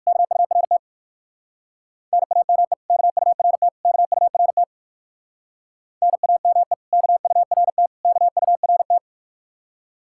Morse code, heard as a keyed tone, about 61 wpm, 700 Hz. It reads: KURT NAME KURT KURT NAME KURT KURT